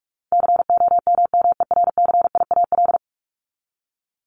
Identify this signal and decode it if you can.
Morse code, heard as a keyed tone, about 44 wpm, 700 Hz. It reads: COMMERCIAL
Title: Top 501-750 Words 45wpm